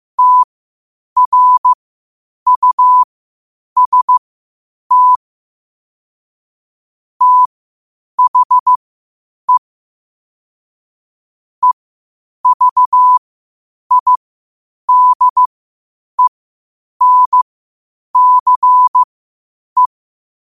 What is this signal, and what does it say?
Morse code, heard as a keyed tone, about 15 wpm, spaced as for 8 wpm, 1000 Hz. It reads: TRUST THE EVIDENCE
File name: qotdfiles / 8wpm